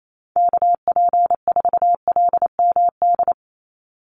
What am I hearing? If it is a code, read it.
Morse code, heard as a keyed tone, about 28 wpm, 700 Hz. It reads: KP4LMD
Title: USA Call Signs 28wpm